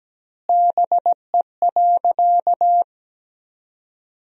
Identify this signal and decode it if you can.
Morse code, heard as a keyed tone, about 17 wpm, 700 Hz. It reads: BE.
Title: 2x Word-Spacing - Sentences from Top 200 Words - 17wpm